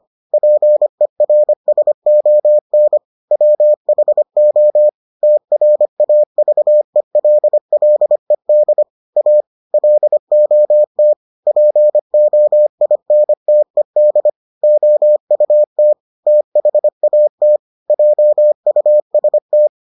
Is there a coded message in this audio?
Yes: Morse code from 0.3 to 19.7 s, heard as a keyed tone, about 25 wpm, 600 Hz. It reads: PERSON WHO TRAVELLED A LOT POINTED OUT THAT JUST